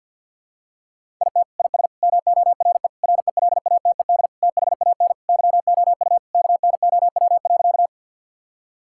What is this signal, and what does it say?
Morse code, heard as a keyed tone, about 51 wpm, 700 Hz. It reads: IT IS MORE RELATED THAN YOU KNOW.
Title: Sentences from Top 1000 Words - Review Part 5 - 50wpm